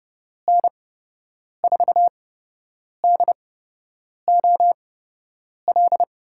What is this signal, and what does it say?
Morse code, heard as a keyed tone, about 30 wpm, spaced as for 8 wpm, 700 Hz. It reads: N4DOL
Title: USA Call Signs - 8x Character Spacing - 30wpm